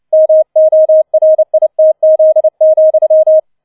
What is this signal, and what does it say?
Morse code, heard as a keyed tone, about 29 wpm, 630 Hz. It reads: MORITZ,